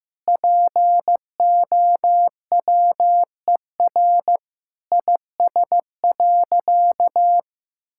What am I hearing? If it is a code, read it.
Morse code, heard as a keyed tone, about 15 wpm, 700 Hz. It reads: POWER IS.
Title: Sentences from Top 1000 Words - Review Part 6 - 15wpm